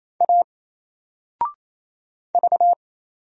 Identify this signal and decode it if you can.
Morse code, heard as a keyed tone, about 27 wpm, 700 Hz. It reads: A V
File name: Single-Letter-Number-28wpm